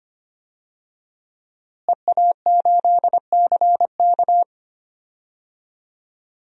Morse code, heard as a keyed tone, about 25 wpm, 700 Hz. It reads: EA8CK
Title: DXCC-100 Call Signs - 5 Characters - Repeated 2x - 25wpm